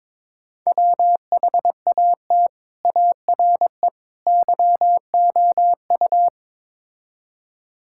Morse code, heard as a keyed tone, about 22 wpm, 700 Hz. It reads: WHAT ARE YOU
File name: practice / 22wpm / Build A Phrase - 5-Word Phrases - 22wpm